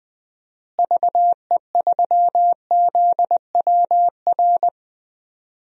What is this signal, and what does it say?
Morse code, heard as a keyed tone, about 20 wpm, 700 Hz. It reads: VE3ZWR
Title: DXCC-100 Call Signs - 6 Characters - Repeated 3x - 20wpm